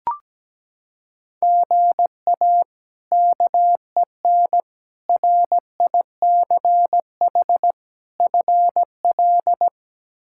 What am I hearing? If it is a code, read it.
Morse code, heard as a keyed tone, about 17 wpm, 700 Hz. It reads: GA KEN RICH FL
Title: K1USN SST QSO Practice - 17wpm